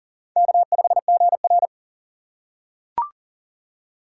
Morse code, heard as a keyed tone, about 40 wpm, 700 Hz. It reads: K5GR
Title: USA Call Signs - 4 Characters - 3x Character Spacing - 40wpm